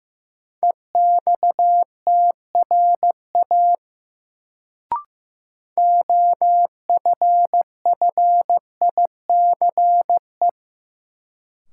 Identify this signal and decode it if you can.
Morse code, heard as a keyed tone, about 15 wpm, 700 Hz. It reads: EXTRA OFFICE
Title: Sentences from Top 1000 Words - New Word Drill - 15wpm